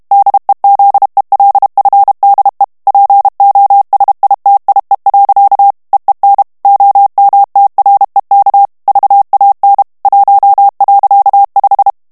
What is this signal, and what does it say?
Morse code, heard as a keyed tone, about 32 wpm, 800 Hz. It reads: DEZELFDE POSITIE. EEN OMTREK VAN 1.5
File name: morse mp3 / HOLLANDAIS texte clair 1 vitesse 30 mots minute